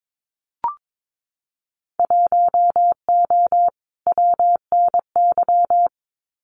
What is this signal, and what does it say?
Morse code, heard as a keyed tone, about 22 wpm, 700 Hz. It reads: 1O WNY
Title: Winter Field Day Exchanges 22wpm